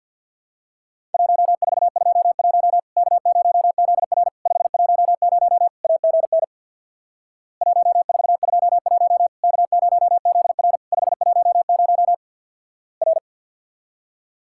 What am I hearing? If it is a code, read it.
Morse code, heard as a keyed tone, about 50 wpm, 700 Hz. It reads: 1421 K0ZR 510 AGN 1421 K0ZR 510 R
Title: Worked All Europe DX Contest - QTC Exchange with Repeats - 50wpm